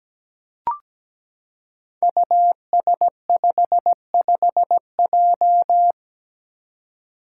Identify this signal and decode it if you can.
Morse code, heard as a keyed tone, about 17 wpm, 700 Hz. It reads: US55J
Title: Five Challenging Letter-Number Combinations 17wpm